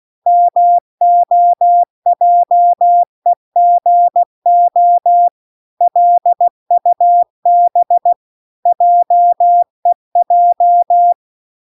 Morse code, heard as a keyed tone, about 16 wpm, 700 Hz. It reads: MOJEGO LUB JEJ